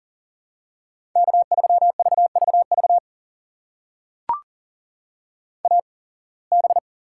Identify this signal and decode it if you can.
Morse code, heard as a keyed tone, about 40 wpm, spaced as for 26 wpm, 700 Hz. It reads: K3VVV AB
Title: USA Call Signs - 5 Characters - 8x Character Spacing - 40wpm